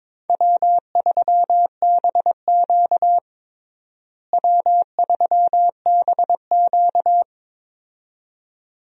Morse code, heard as a keyed tone, about 22 wpm, 700 Hz. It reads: W3BQ W3BQ
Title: USA Call Signs - 4 Characters - Repeated 2x - 22wpm